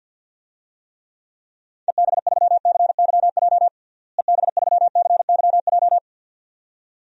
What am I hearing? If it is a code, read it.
Morse code, heard as a keyed tone, about 48 wpm, 700 Hz. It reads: EB3CYJ EB3CYJ
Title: DXCC-100 Call Signs - 6 Characters - Repeated 2x - 50wpm